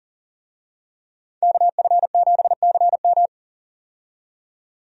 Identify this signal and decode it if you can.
Morse code, heard as a keyed tone, about 40 wpm, 700 Hz. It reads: KF7CM
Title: USA Call Signs - 5 Characters - 40wpm